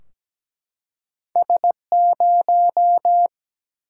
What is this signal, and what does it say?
Morse code, heard as a keyed tone, about 17 wpm, 700 Hz. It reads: S0
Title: Single-Letter Single-Number 17wpm